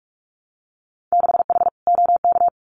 Morse code, heard as a keyed tone, about 44 wpm, 700 Hz. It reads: 6H OK
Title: Winter Field Day Exchanges Encore 45wpm